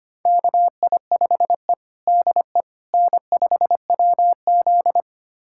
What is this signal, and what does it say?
Morse code, heard as a keyed tone, about 25 wpm, 700 Hz. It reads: KI5E DE N5WZ
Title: USA Callsign DE Callsign - 4 Characters - 25wpm